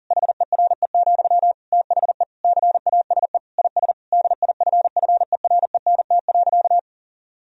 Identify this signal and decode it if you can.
Morse code, heard as a keyed tone, about 40 wpm, 700 Hz. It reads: HERE, THE CASE IS DIFFERENT.
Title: Sentences from Top 1000 Words - Review Part 3 - 40wpm